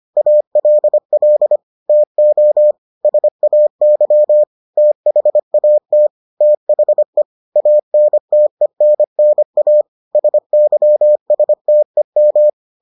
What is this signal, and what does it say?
Morse code, heard as a keyed tone, about 25 wpm, 600 Hz. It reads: ALL TO SAY THAT THE ANTENNA SYSTEM